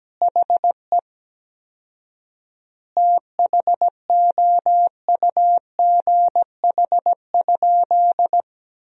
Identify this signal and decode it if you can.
Morse code, heard as a keyed tone, about 17 wpm, 700 Hz. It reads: HE THOUGH?
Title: 4x Word-Spacing - Sentences from Top 400 Words - 17wpm